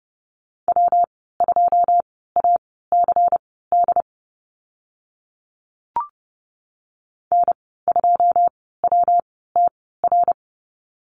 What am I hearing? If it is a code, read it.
Morse code, heard as a keyed tone, about 30 wpm, spaced as for 17 wpm, 700 Hz. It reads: W2ACD N2WTR
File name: Lesson 19g - Letter w - Call Signs 30wpm